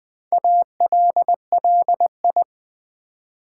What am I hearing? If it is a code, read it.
Morse code, heard as a keyed tone, about 20 wpm, 700 Hz. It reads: ALLI
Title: Build A Word - 2 Letters At A Time - 8-Letter Words - 20wpm